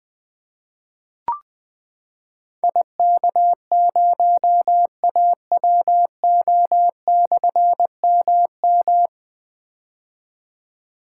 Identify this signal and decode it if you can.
Morse code, heard as a keyed tone, about 20 wpm, 700 Hz. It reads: IK0AWO/MM
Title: Difficult Call Signs 20wpm